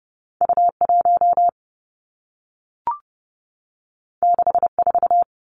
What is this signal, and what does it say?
Morse code, heard as a keyed tone, about 30 wpm, 700 Hz. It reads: U 1 6 4
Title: Two Challenging Letter-Number Combinations 30wpm